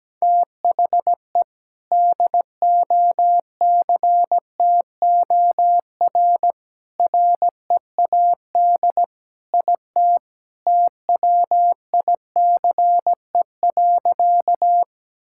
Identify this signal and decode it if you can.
Morse code, heard as a keyed tone, about 17 wpm, 700 Hz. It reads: THE DOCTOR READ IT TWICE.
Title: Sentences from Top 1600 Words 17wpm